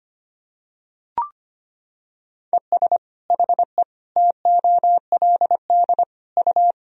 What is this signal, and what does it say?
Morse code, heard as a keyed tone, about 25 wpm, 700 Hz. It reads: ES HE TOLD U
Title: Sentences from Top 500 Words and Abbreviations 25wpm